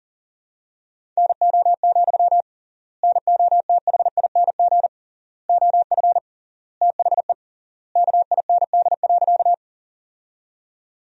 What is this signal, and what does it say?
Morse code, heard as a keyed tone, about 37 wpm, 700 Hz. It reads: NO, NOTHING OF THE KIND.
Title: 3x Word-Spacing - Sentences from Top 500 Words - 40wpm